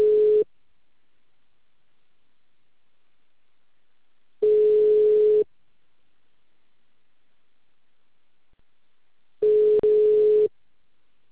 Ringback tone (Europe, 425 Hz).